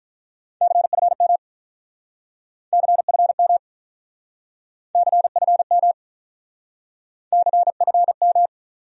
Morse code, heard as a keyed tone, about 47 wpm, 700 Hz. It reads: CFM CFM CFM CFM